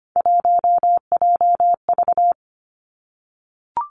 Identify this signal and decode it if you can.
Morse code, heard as a keyed tone, about 25 wpm, 700 Hz. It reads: 1JV
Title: Three Challenging Letter-Number Combinations 25wpm